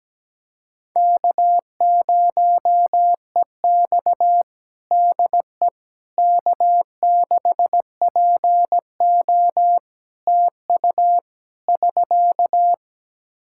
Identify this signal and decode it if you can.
Morse code, heard as a keyed tone, about 17 wpm, 700 Hz. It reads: K0EX DE K6PO TU <SK>
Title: Practice QSO - Exchange 4 - Ending - 17wpm